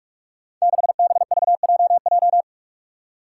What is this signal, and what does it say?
Morse code, heard as a keyed tone, about 44 wpm, 700 Hz. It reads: 6BVJJ
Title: Five Challenging Letter-Number Combinations 45wpm